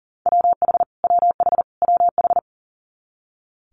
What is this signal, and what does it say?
Morse code, heard as a keyed tone, about 40 wpm, 690 Hz. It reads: WH WH WH
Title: Multi-letter Phonemes Repeated 3x 40wpm